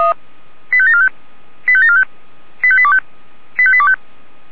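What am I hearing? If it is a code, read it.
Five-tone selective call (ZVEI), heard as quick runs of five tones, tones of about 70 ms. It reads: 85745 85745 85825 85825
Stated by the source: Five-tone ZVEI selective call received on an SDR.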